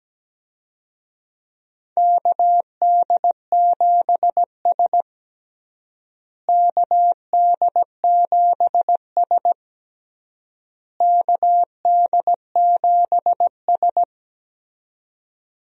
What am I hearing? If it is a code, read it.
Morse code, heard as a keyed tone, about 17 wpm, 700 Hz. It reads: KD7S KD7S KD7S
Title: USA Call Signs - 4 Characters - Repeated 3x - 17wpm